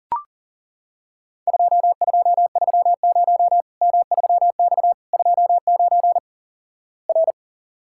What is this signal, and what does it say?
Morse code, heard as a keyed tone, about 37 wpm, 700 Hz. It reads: 2230 M3X 29 R
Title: Worked All Europe DX Contest - QTC Exchange - 40wpm